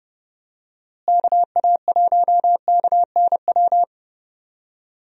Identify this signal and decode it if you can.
Morse code, heard as a keyed tone, about 31 wpm, 700 Hz. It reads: KA1KNW
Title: USA Call Signs - 6 Characters - Repeated 2x - 30wpm